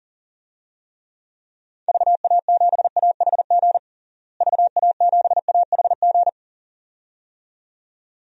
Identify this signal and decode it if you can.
Morse code, heard as a keyed tone, about 40 wpm, 700 Hz. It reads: VA7AHG VA7AHG
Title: DXCC-100 Call Signs - 6 Characters - Repeated 2x - 40wpm